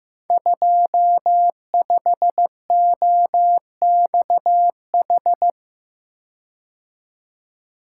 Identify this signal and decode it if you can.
Morse code, heard as a keyed tone, about 15 wpm, 700 Hz. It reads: 25OXH
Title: Code Groups - 5 Letter:Numbers - 15wpm